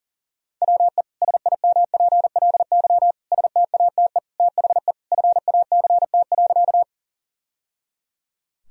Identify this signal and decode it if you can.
Morse code, heard as a keyed tone, about 40 wpm, 700 Hz. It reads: WE SIMPLY STATE THE FACT.
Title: Sentences from Top 1000 Words - Review Part 4 - 40wpm